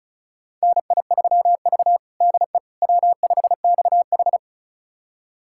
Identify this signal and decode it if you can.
Morse code, heard as a keyed tone, about 34 wpm, 700 Hz. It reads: NI3V DE W5XH
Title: USA Callsign DE Callsign - 4 Characters - 35wpm